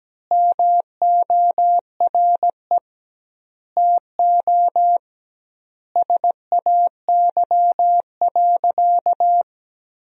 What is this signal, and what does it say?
Morse code, heard as a keyed tone, about 17 wpm, 700 Hz. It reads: MORE TO SAY.
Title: 2x Word-Spacing - Sentences from Top 500 Words - 17wpm